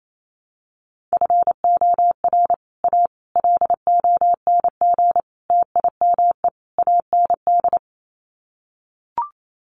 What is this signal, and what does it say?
Morse code, heard as a keyed tone, about 28 wpm, 700 Hz. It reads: FOR A LONG TIME AND
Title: Sets of 5 Words Encore2 28wpm